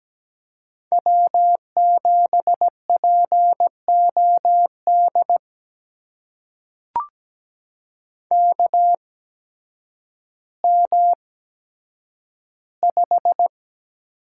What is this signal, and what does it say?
Morse code, heard as a keyed tone, about 17 wpm, spaced as for 9 wpm, 700 Hz. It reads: W7POD KM5
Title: USA Call Signs - 5 Characters - 8x Character Spacing - 17wpm